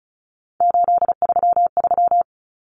Morse code, heard as a keyed tone, about 34 wpm, 700 Hz. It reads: 833
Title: Common Cut Numbers - 3-Digit 35wpm